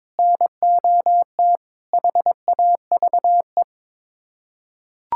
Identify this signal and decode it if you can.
Morse code, heard as a keyed tone, about 22 wpm, 700 Hz. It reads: NOT HAVE
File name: practice / 22wpm / Build A Phrase - 5-Word Phrases - 22wpm